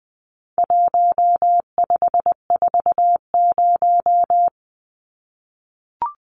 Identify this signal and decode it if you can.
Morse code, heard as a keyed tone, about 20 wpm, 700 Hz. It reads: 1540
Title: Numbers - 4-Digit 20wpm